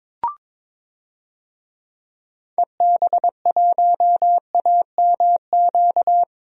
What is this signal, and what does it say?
Morse code, heard as a keyed tone, about 22 wpm, 700 Hz. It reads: EB1AMQ
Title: DXCC-100 Call Signs - 6 Characters - 22wpm